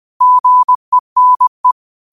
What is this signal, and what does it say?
Morse code, heard as a keyed tone, about 20 wpm, 1000 Hz. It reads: GENE
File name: qotdfiles / 20wpm